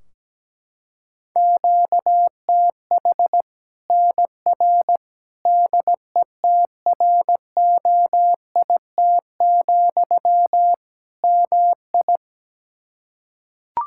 Morse code, heard as a keyed tone, about 17 wpm, 700 Hz. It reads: QTH NR DETROIT, MI
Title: K1USN SST QSO Practice 17wpm